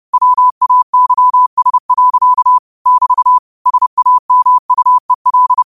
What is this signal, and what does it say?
Morse code, heard as a keyed tone, about 30 wpm, 1000 Hz. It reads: WAYS. <BT> SAMUEL
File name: qotdfiles / 30wpm